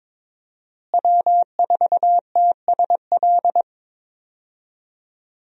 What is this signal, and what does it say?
Morse code, heard as a keyed tone, about 22 wpm, 700 Hz. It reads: W4TSL
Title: USA Call Signs - 5 Characters - Repeated 2x - 22wpm